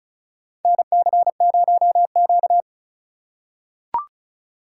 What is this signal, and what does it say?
Morse code, heard as a keyed tone, about 34 wpm, 700 Hz. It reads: NC0Q
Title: USA Call Signs - 3x Character Spacing - 35wpm